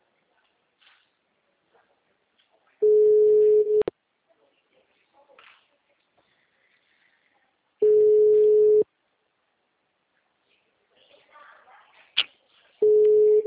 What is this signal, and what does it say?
Ringback tone (Europe, 425 Hz).